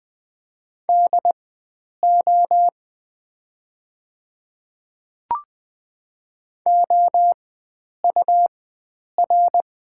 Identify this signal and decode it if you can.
Morse code, heard as a keyed tone, about 20 wpm, spaced as for 9 wpm, 700 Hz. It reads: DO OUR
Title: 4x Character Spacing - Top 300 Words - 20wpm